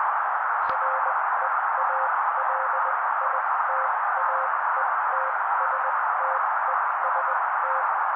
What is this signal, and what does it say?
Morse code, heard as a keyed tone, about 20 wpm, spaced as for 16 wpm, 530 Hz. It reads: REALITAETSTEST
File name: daily / 16wpm noise